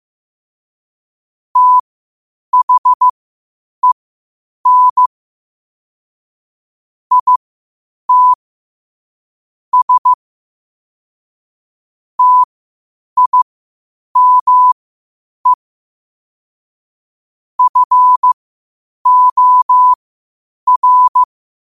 Morse code, heard as a keyed tone, about 15 wpm, spaced as for 8 wpm, 1000 Hz. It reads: THEN IT S TIME FOR
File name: qotdfiles / 8wpm